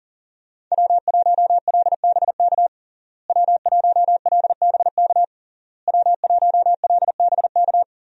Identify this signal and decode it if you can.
Morse code, heard as a keyed tone, about 40 wpm, 700 Hz. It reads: W1LBK W1LBK W1LBK
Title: USA Call Signs - 5 Characters - Repeated 3x - 40wpm